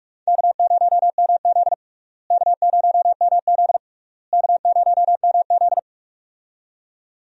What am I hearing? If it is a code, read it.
Morse code, heard as a keyed tone, about 46 wpm, 700 Hz. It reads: K0MZ K0MZ K0MZ
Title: USA Call Signs - 4 Characters - Repeated 3x - 45wpm